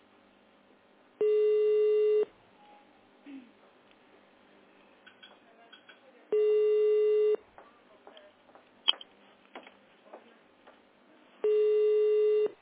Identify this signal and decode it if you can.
Ringback tone (Europe, 425 Hz).